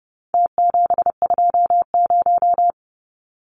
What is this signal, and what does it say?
Morse code, heard as a keyed tone, about 30 wpm, 700 Hz. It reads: T720
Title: Numbers - Contest Style - 4-Digit 30wpm